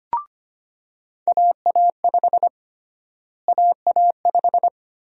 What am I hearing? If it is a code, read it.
Morse code, heard as a keyed tone, about 25 wpm, 700 Hz. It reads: AA5 AA5
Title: USA Call Sign Prefixes - Repeated 2x - 25wpm